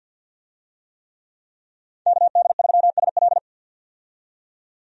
Morse code, heard as a keyed tone, about 46 wpm, 700 Hz. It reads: KD3SL
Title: USA Call Signs - 5 Characters - 50wpm